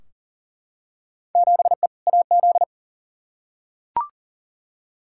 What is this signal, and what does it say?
Morse code, heard as a keyed tone, about 38 wpm, 700 Hz. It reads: 7E AZ
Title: Field Day Exchanges Encore - 40wpm